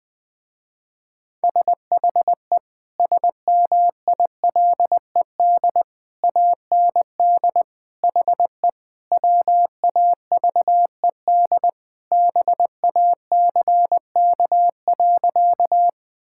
Morse code, heard as a keyed tone, about 21 wpm, 700 Hz. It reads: SHE SMILED AND HE WAVED BACK.
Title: Sentences with 2-Word Phrases - Spoken Beginning - 20wpm